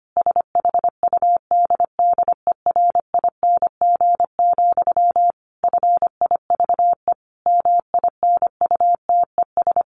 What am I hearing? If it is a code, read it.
Morse code, heard as a keyed tone, about 25 wpm, 700 Hz. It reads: SHUDDERING, FIVE MINUTES
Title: A Sound Of Thunder 25wpm